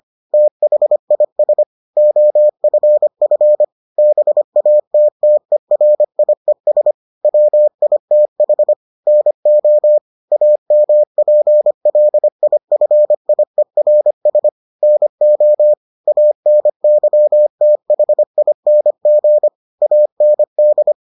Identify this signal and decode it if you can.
Morse code, heard as a keyed tone, about 25 wpm, 600 Hz. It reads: THIS OFF BATTERIES WITH NO AMPLIFIERS NO ANYTHING AND